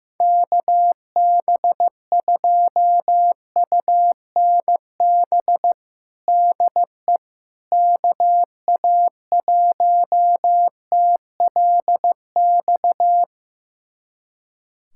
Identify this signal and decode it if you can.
Morse code, heard as a keyed tone, about 15 wpm, 700 Hz. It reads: KB2UNB DE KA1TLX
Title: USA Callsign DE Callsign - 6 Characters - 15wpm